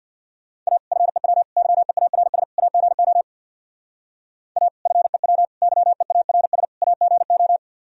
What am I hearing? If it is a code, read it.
Morse code, heard as a keyed tone, about 64 wpm, 700 Hz. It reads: A FEW YEARS AGO A FEW YEARS AGO